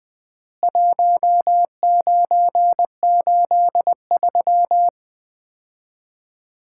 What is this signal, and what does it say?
Morse code, heard as a keyed tone, about 20 wpm, 700 Hz. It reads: 1983
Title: Common QSO Phrases - 3x Word-Spacing - 20wpm